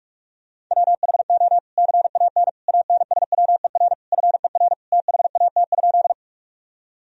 Morse code, heard as a keyed tone, about 46 wpm, 700 Hz. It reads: WHO CAN ANSWER FER THAT?
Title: Sentences from Top 300 Words and Abbreviations 45wpm